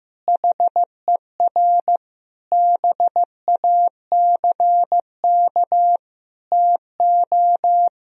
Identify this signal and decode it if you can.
Morse code, heard as a keyed tone, about 15 wpm, 700 Hz. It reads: HER BACK TO
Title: Sentences from Top 1000 Words - Review Part 3 - 15wpm